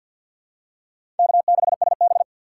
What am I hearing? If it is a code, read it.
Morse code, heard as a keyed tone, about 51 wpm, 700 Hz. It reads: K6SB
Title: USA Call Signs - 4 Characters - 2x Character Spacing - 50wpm